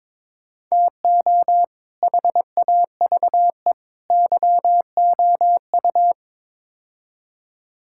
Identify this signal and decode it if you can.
Morse code, heard as a keyed tone, about 22 wpm, 700 Hz. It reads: TO HAVE YOU
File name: practice / 22wpm / Build A Phrase - 5-Word Phrases - 22wpm